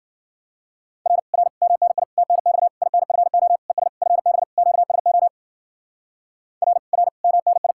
Morse code, heard as a keyed tone, about 51 wpm, 700 Hz. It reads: R R MNI TNX INFO ES FB QSO R R MNI